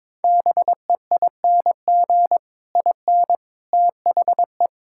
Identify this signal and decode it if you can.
Morse code, heard as a keyed tone, about 22 wpm, 700 Hz. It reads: BEING IN THE
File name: Sets of 9 Words - 22wpm